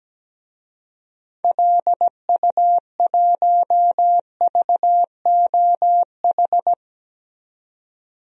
Morse code, heard as a keyed tone, about 17 wpm, 700 Hz. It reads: LU1VOH
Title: DXCC-100 Call Signs - 6 Characters - 17wpm